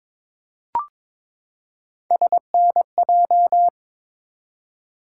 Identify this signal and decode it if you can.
Morse code, heard as a keyed tone, about 21 wpm, 700 Hz. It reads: SNJ
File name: ARRL Sections v2 - 22wpm